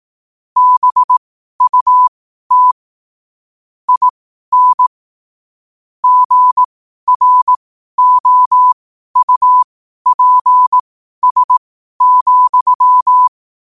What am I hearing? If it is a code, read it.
Morse code, heard as a keyed tone, about 18 wpm, spaced as for 12 wpm, 1000 Hz. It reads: BUT IN GROUPS,
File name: qotdfiles / 12wpm